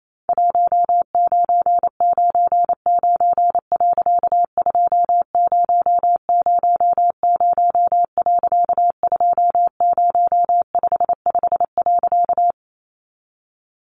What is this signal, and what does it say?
Morse code, heard as a keyed tone, about 28 wpm, 700 Hz. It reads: 1999. 2000. 2055.
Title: A Sound Of Thunder 28wpm